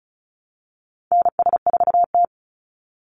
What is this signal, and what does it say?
Morse code, heard as a keyed tone, about 34 wpm, 700 Hz. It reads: NS4T
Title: Call Signs Repeated 3x - 35wpm